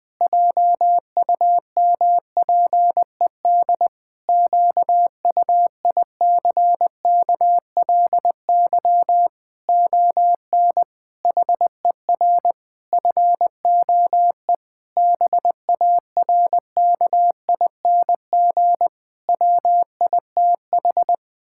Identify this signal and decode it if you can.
Morse code, heard as a keyed tone, about 20 wpm, 700 Hz. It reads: JUMPED QUICKLY ON HER FOE BARKING WITH
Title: Pangram Challenge v2 - 20wpm